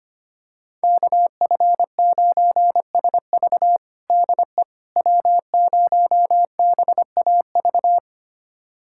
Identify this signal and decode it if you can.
Morse code, heard as a keyed tone, about 25 wpm, 700 Hz. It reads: KF9SV DE W0BAV
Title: USA Callsign DE Callsign - 5 Characters - 25wpm